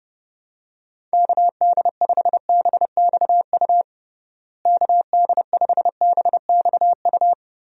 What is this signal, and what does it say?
Morse code, heard as a keyed tone, about 30 wpm, 700 Hz. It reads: KD5BXU KD5BXU
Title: USA Call Signs - 6 Characters - Repeated 3x - 30wpm